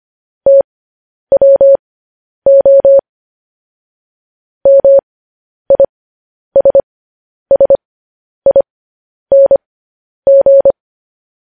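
Morse code, heard as a keyed tone, about 25 wpm, spaced as for 10 wpm, 550 Hz. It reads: TWO MISSING
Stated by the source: Morse code news headlines sent as a reading transcription, preceded by vvv.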